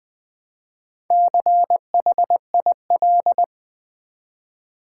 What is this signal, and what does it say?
Morse code, heard as a keyed tone, about 20 wpm, 700 Hz. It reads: CHIL
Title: Build A Word - 1 Letter At A Time - 5-Letter Words - 20wpm